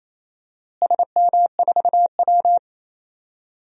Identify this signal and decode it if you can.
Morse code, heard as a keyed tone, about 28 wpm, 700 Hz. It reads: SM4W
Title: DXCC-100 Call Signs - 4 Characters - Repeated 2x - 28wpm